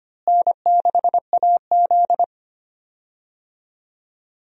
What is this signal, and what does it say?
Morse code, heard as a keyed tone, about 25 wpm, 700 Hz. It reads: N6AZ
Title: USA Call Signs - 4 Characters - Repeated 2x - 25wpm